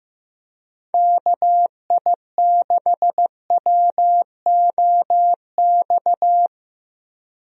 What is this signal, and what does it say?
Morse code, heard as a keyed tone, about 16 wpm, 700 Hz. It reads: KI6WOX
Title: USA Call Signs - 6 Characters - Repeated 2x - 15wpm